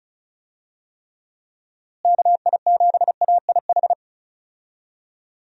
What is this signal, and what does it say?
Morse code, heard as a keyed tone, about 36 wpm, 700 Hz. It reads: KI7AIH
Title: USA Call Signs - 6 Characters - Repeated 3x - 35wpm